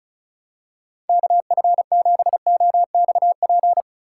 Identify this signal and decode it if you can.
Morse code, heard as a keyed tone, about 36 wpm, 700 Hz. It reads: KF7OXP
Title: USA Call Signs - 6 Characters - 2x Character Spacing - 35wpm